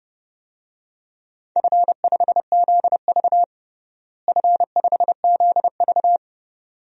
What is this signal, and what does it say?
Morse code, heard as a keyed tone, about 30 wpm, 700 Hz. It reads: F5ZV F5ZV
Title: DXCC-100 Call Signs - 4 Characters - Repeated 2x - 30wpm